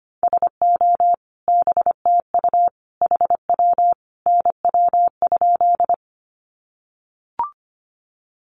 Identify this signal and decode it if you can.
Morse code, heard as a keyed tone, about 25 wpm, 700 Hz. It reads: SO BTU HW NW?
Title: Common QSO Phrases 25wpm